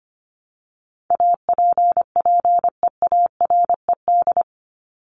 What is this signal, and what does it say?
Morse code, heard as a keyed tone, about 25 wpm, 700 Hz. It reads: APPEARED
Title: ED Suffix - Top 100 Words with Base Words 25wpm